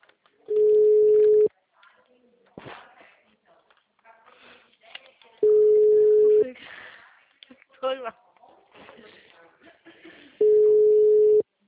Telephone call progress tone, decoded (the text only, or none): ringback tone